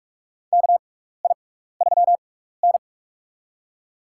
Morse code, heard as a keyed tone, about 47 wpm, spaced as for 16 wpm, 700 Hz. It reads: KI3N
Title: USA Call Signs - 4 Characters - 6x Character Spacing - 45wpm